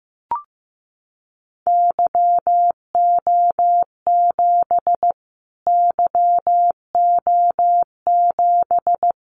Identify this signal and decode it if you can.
Morse code, heard as a keyed tone, about 15 wpm, 700 Hz. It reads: YO7 YO7
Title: Callsign Prefixes for Top 25 DXCC - Repeated 3x - 15wpm